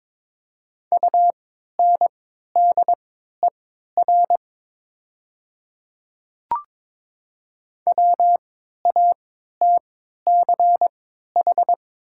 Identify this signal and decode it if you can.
Morse code, heard as a keyed tone, about 22 wpm, spaced as for 12 wpm, 690 Hz. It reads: UNDER WATCH
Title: 3x Character Spacing - Top 300 Words - 22wpm